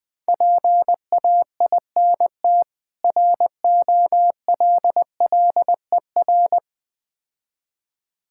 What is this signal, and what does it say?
Morse code, heard as a keyed tone, about 20 wpm, 690 Hz. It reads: PAINT ROLLER
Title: Difficult Words with Hints - 20wpm